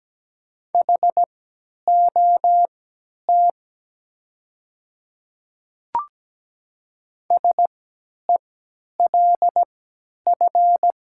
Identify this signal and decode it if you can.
Morse code, heard as a keyed tone, about 17 wpm, spaced as for 10 wpm, 700 Hz. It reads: HOT SELF
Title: 3x Character Spacing - Top 200 Words - 17wpm